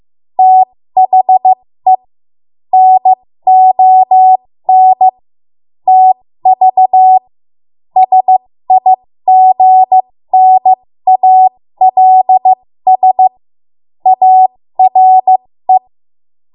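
Morse code, heard as a keyed tone, about 15 wpm, spaced as for 13 wpm, 750 Hz. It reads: THE NON TV SIGNALS ARE
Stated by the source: 13 wpm Morse code practice archive transmission.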